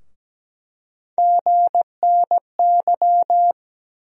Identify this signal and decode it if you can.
Morse code, heard as a keyed tone, about 17 wpm, 700 Hz. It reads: GNY
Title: USA Call Sign Postfixes - 3 Letters 17wpm